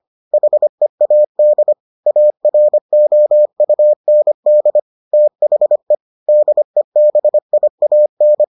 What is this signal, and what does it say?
Morse code, heard as a keyed tone, about 25 wpm, 600 Hz. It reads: HEAD AROUND THE DEBIAN